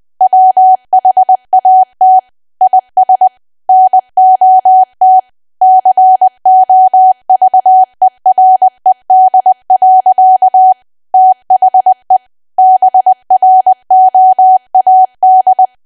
Morse code, heard as a keyed tone, about 20 wpm, 750 Hz. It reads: WHAT IS NOT COVERED. THE BROAD
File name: cwa / W1AW-Level3 / 20WPM